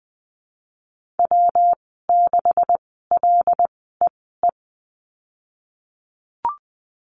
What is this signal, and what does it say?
Morse code, heard as a keyed tone, about 20 wpm, spaced as for 15 wpm, 700 Hz. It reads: W6LEE
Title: 179 Lesson 29g - Slash - Call Signs 20wpm